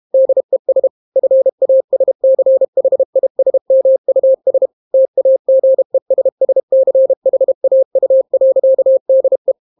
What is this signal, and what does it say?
Morse code, heard as a keyed tone, about 32 wpm, 530 Hz. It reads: DES FASCHISMUS TAGESSCHAU.DE